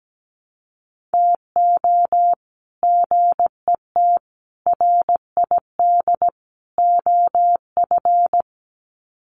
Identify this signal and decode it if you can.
Morse code, heard as a keyed tone, about 17 wpm, 700 Hz. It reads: TO GET RID OF
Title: Sets of 4 Words 17wpm